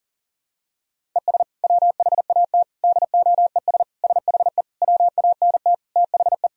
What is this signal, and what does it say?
Morse code, heard as a keyed tone, about 40 wpm, 700 Hz. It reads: ES WHAT DOES SHE WANT THE
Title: Sentences from Top 400 Words and Abbreviations 40wpm